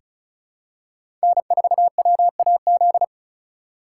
Morse code, heard as a keyed tone, about 36 wpm, 700 Hz. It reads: N4WAZ
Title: USA Call Signs - 5 Characters - 2x Character Spacing - 35wpm